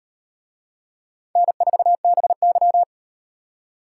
Morse code, heard as a keyed tone, about 38 wpm, 700 Hz. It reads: N4BY